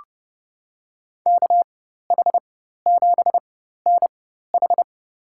Morse code, heard as a keyed tone, about 30 wpm, spaced as for 14 wpm, 700 Hz. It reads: KH7NH
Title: USA Call Signs - 5 Characters - 4x Character Spacing - 30wpm